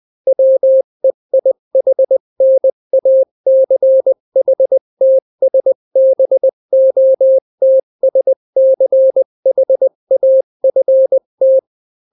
Morse code, heard as a keyed tone, about 20 wpm, 530 Hz. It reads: WEIHNACHTSBOTSCHAFT
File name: daily / 18wpm